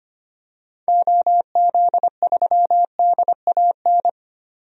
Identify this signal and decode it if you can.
Morse code, heard as a keyed tone, about 25 wpm, 690 Hz. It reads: OZ3DAN
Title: DXCC-100 Call Signs - 6 Characters - 25wpm